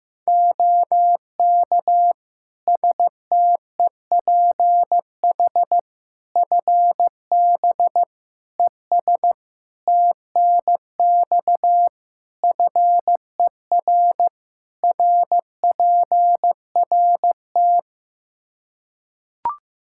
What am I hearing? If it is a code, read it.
Morse code, heard as a keyed tone, about 16 wpm, 700 Hz. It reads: OK STEPH FB ES TNX FER RPRT
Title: Practice QSO - Exchange 2 - Info - 15wpm